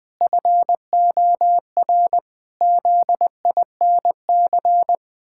Morse code, heard as a keyed tone, about 20 wpm, 700 Hz. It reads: FOR ZINC
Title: Pangram Challenge v2 - 20wpm